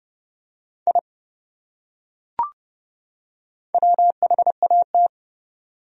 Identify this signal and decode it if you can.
Morse code, heard as a keyed tone, about 27 wpm, 700 Hz. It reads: I WHAT